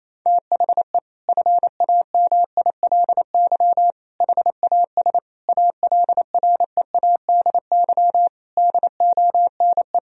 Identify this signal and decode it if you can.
Morse code, heard as a keyed tone, about 28 wpm, 700 Hz. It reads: THE FAMILY HAS ALREADY DONE
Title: Sentences from Top 1000 Words - Review Part 2 - 28wpm